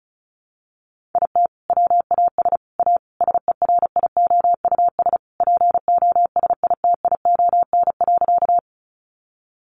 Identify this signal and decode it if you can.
Morse code, heard as a keyed tone, about 34 wpm, 700 Hz. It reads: IT WAS A SERIOUS POSITION.
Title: Sentences from Top 1100 Words 35wpm